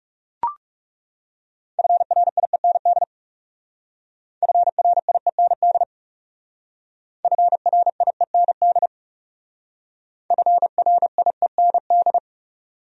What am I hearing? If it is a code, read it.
Morse code, heard as a keyed tone, about 38 wpm, 700 Hz. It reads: FRIEND FRIEND FRIEND FRIEND